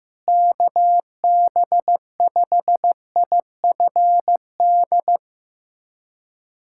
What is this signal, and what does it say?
Morse code, heard as a keyed tone, about 15 wpm, 700 Hz. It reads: KB5IFD
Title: USA Callsign DE Callsign 15wpm